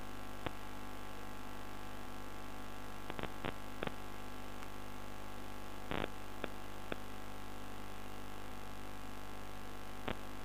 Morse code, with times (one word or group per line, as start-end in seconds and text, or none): none